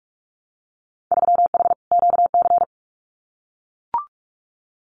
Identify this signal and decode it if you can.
Morse code, heard as a keyed tone, about 44 wpm, 700 Hz. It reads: 3H QC
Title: Winter Field Day Exchanges 45wpm